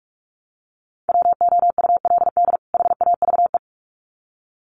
Morse code, heard as a keyed tone, about 46 wpm, 700 Hz. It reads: WOULD HAVE
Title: Sets of 2 Words 45wpm